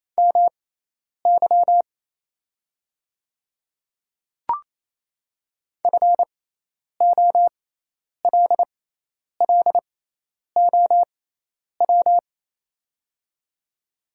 Morse code, heard as a keyed tone, about 27 wpm, spaced as for 10 wpm, 700 Hz. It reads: MY FOLLOW
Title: 6x Character Spacing - Top 200 Words - 28wpm